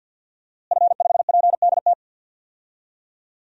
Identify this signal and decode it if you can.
Morse code, heard as a keyed tone, about 52 wpm, 700 Hz. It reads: F5PDT
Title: DXCC-100 Call Signs - 5 Characters - Repeated 2x - 50wpm